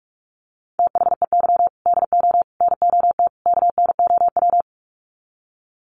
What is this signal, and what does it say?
Morse code, heard as a keyed tone, about 44 wpm, 700 Hz. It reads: THEY DO NOT KNOW
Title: Sets of 4 Words 45wpm